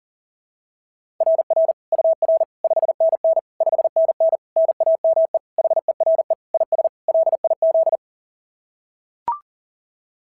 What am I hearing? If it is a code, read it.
Morse code, heard as a keyed tone, about 37 wpm, 690 Hz. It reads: RR UR 5NN 5NN NAME HERE IS LIZ
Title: SOTA QSO Practice 40wpm